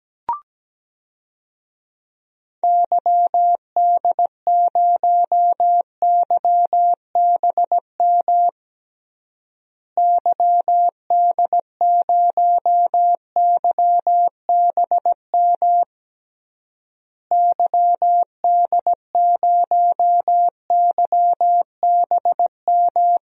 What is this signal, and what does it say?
Morse code, heard as a keyed tone, about 17 wpm, 700 Hz. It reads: YD0YBM YD0YBM YD0YBM
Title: DXCC-100 Call Signs - 6 Characters - Repeated 3x - 17wpm